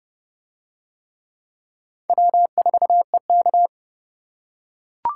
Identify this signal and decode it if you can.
Morse code, heard as a keyed tone, about 29 wpm, 700 Hz. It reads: W4EK